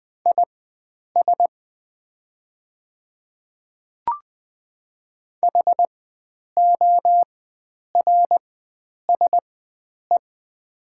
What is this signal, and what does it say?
Morse code, heard as a keyed tone, about 20 wpm, spaced as for 9 wpm, 700 Hz. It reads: IS HORSE
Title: 4x Character Spacing - Top 300 Words - 20wpm